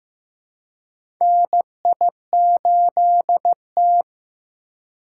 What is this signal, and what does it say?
Morse code, heard as a keyed tone, about 15 wpm, 700 Hz. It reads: NI8T
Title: USA Call Signs - 4 Characters - 6x Character Spacing - 15wpm